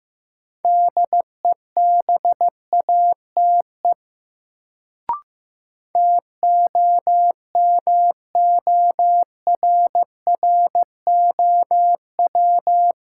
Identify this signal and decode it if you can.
Morse code, heard as a keyed tone, about 15 wpm, 700 Hz. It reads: DEBATE TOMORROW
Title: Sentences from Top 1100 Words - New Word Drill - 15wpm